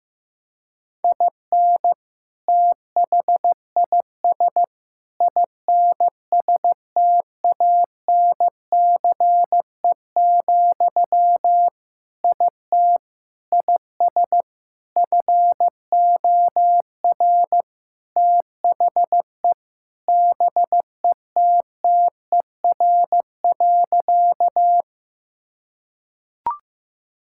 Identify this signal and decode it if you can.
Morse code, heard as a keyed tone, about 15 wpm, 700 Hz. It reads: IN THIS INSTANCE, IT IS FOR THE BETTER.
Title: Sentences from Top 1000 Words - Review Part 5 - 15wpm